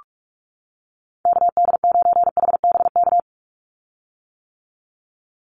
Morse code, heard as a keyed tone, about 44 wpm, 700 Hz. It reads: KD9HBK
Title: USA Call Signs 45wpm